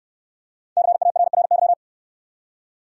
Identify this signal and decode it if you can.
Morse code, heard as a keyed tone, about 65 wpm, 700 Hz. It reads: CARRY